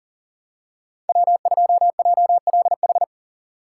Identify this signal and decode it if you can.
Morse code, heard as a keyed tone, about 40 wpm, 700 Hz. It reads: W2JLH
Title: USA Call Signs - 5 Characters - 4x Character Spacing - 40wpm